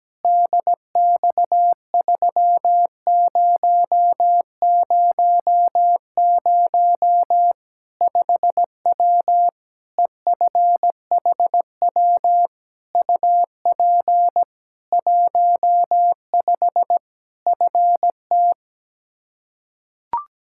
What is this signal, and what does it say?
Morse code, heard as a keyed tone, about 17 wpm, 700 Hz. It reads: DX3000 5W EFHW UP 15 FT
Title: K1USN SST QSO Practice 17wpm